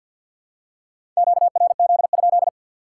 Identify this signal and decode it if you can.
Morse code, heard as a keyed tone, about 52 wpm, 700 Hz. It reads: QRZ?
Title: Top 180 QSO Elements 50wpm